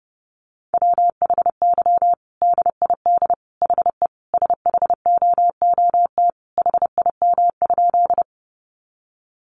Morse code, heard as a keyed tone, about 30 wpm, 700 Hz. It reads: WHY DID HE SHOOT HIM?
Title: Sentences from Top 1200 Words 30wpm